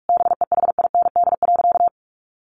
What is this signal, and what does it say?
Morse code, heard as a keyed tone, about 46 wpm, 700 Hz. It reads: BEHIND.
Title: A Sound Of Thunder 45wpm